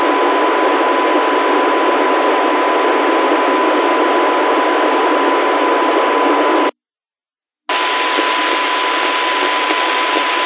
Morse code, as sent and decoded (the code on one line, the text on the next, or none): none